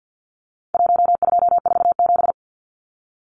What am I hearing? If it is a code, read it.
Morse code, heard as a keyed tone, about 50 wpm, 700 Hz. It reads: 1247
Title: Numbers - Contest Style - 4-Digit 50wpm